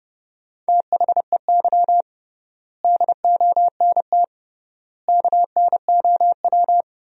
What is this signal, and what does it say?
Morse code, heard as a keyed tone, about 30 wpm, 700 Hz. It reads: THEY DONT KNOW
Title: 3x Word-Spacing - Sentences from Top 300 Words - 30wpm